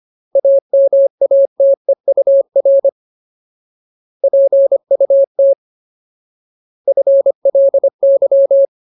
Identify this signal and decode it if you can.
Morse code, heard as a keyed tone, about 25 wpm, 550 Hz. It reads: AMATEUR PUT FLY
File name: Ultimate Word List / 25WPM Ultimate